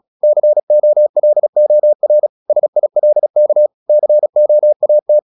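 Morse code, heard as a keyed tone, about 36 wpm, 600 Hz. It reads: COLOR SILK COAT
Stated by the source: Morse practice at 36 wpm, text of Ken Follett's novel Never, from CW med Gnister.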